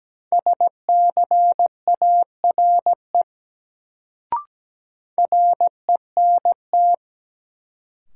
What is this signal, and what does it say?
Morse code, heard as a keyed tone, about 17 wpm, 700 Hz. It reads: SCARE RENT
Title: Sentences from Top 1000 Words - Review Part 5 - New Word Drill - 17wpm